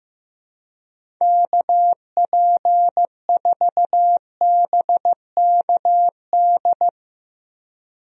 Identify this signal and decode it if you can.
Morse code, heard as a keyed tone, about 15 wpm, 700 Hz. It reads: KP4BKD
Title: DXCC-100 Call Signs - 6 Characters - Repeated 3x - 15wpm